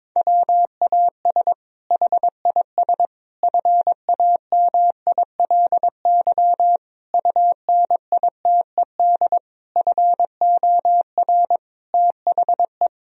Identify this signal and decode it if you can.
Morse code, heard as a keyed tone, about 22 wpm, 700 Hz. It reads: WAS HIS FAMILY UNITED FOR THE
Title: Sentences from Top 1000 Words - Review Part 4 - 22wpm